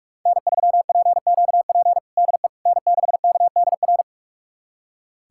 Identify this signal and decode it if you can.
Morse code, heard as a keyed tone, about 46 wpm, 700 Hz. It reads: N3PQP DE N6KDR
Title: USA Callsign DE Callsign - 5 Characters - 45wpm